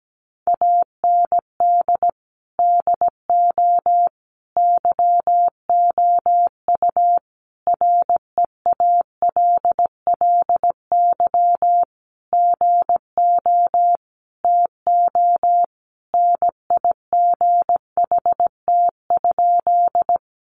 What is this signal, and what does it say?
Morse code, heard as a keyed tone, about 17 wpm, 700 Hz. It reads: AND DO YOU REALLY GO TO NIGHT?
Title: Sentences from Top 600 Words 17wpm